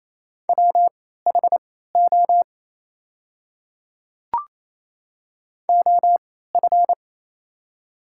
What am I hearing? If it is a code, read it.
Morse code, heard as a keyed tone, about 27 wpm, spaced as for 16 wpm, 690 Hz. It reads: WHO OF
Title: 3x Character Spacing - Top 100 Words - 28wpm